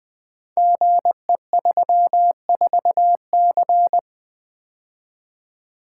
Morse code, heard as a keyed tone, about 20 wpm, 700 Hz. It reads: GE34C
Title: Code Groups - 5 Letter:Numbers - 20wpm